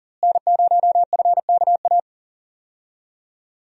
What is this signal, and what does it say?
Morse code, heard as a keyed tone, about 40 wpm, 700 Hz. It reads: N0FKA
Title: USA Call Signs - 5 Characters - Repeated 3x - 40wpm